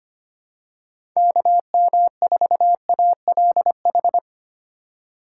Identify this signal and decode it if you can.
Morse code, heard as a keyed tone, about 25 wpm, 700 Hz. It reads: KM4ALH
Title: USA Call Signs - 4x Character Spacing - 25wpm